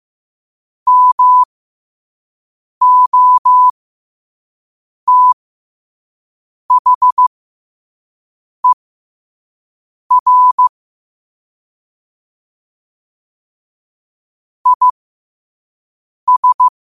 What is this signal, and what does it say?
Morse code, heard as a keyed tone, about 15 wpm, spaced as for 5 wpm, 1000 Hz. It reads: MOTHER IS
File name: qotdfiles / 5wpm